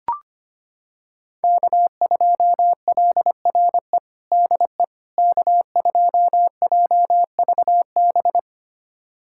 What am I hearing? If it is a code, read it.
Morse code, heard as a keyed tone, about 25 wpm, 700 Hz. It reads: K2LRE DE K2JVB
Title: USA Callsign DE Callsign - 5 Characters - 25wpm